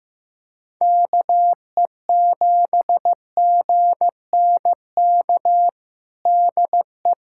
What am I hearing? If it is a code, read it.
Morse code, heard as a keyed tone, about 16 wpm, 700 Hz. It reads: KE7GNK DE
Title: USA Callsign DE Callsign - 6 Characters - 15wpm